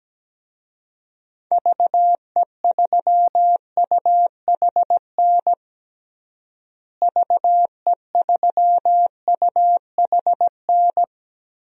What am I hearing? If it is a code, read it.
Morse code, heard as a keyed tone, about 17 wpm, 700 Hz. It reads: VE3UHN VE3UHN
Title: DXCC-100 Call Signs - 6 Characters - Repeated 2x - 17wpm